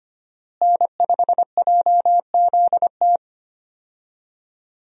Morse code, heard as a keyed tone, about 25 wpm, 700 Hz. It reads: N5JZT
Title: USA Call Signs - 5 Characters - Repeated 2x - 25wpm